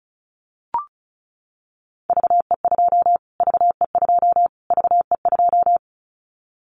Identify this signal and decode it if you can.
Morse code, heard as a keyed tone, about 34 wpm, 700 Hz. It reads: VE2 VE2 VE2
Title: Callsign Prefixes for Top 25 DXCC - Repeated 3x - 35wpm